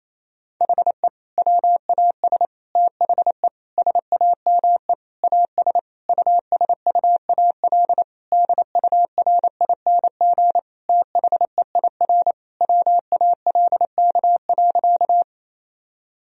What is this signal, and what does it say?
Morse code, heard as a keyed tone, about 28 wpm, 700 Hz. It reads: HE WAS THE SAME AS USUAL DURING THEIR WALK.
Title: Sentences from Top 1000 Words - Review Part 2 - 28wpm